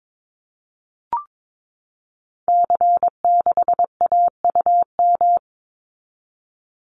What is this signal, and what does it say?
Morse code, heard as a keyed tone, about 22 wpm, 700 Hz. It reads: C6AUM
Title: Call Signs Repeated 3x - 22wpm